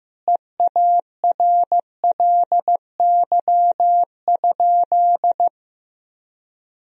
Morse code, heard as a keyed tone, about 15 wpm, 700 Hz. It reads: EARLY?
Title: Sentences from Top 1000 Words - Review Part 2 - 15wpm